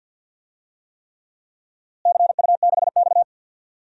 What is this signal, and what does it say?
Morse code, heard as a keyed tone, about 46 wpm, 700 Hz. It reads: CU6X
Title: DXCC-100 Call Signs - 4 Characters - Repeated 3x - 50wpm